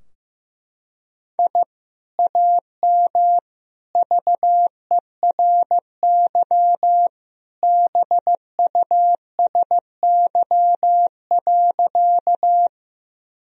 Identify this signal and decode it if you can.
Morse code, heard as a keyed tone, about 16 wpm, 700 Hz. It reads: I AM VERY BUSY.
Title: Sentences from Top 1000 Words - Review Part 3 - 15wpm